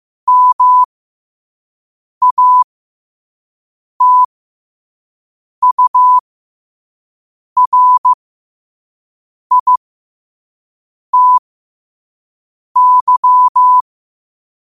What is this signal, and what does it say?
Morse code, heard as a keyed tone, about 15 wpm, spaced as for 5 wpm, 1000 Hz. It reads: MATURITY
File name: qotdfiles / 5wpm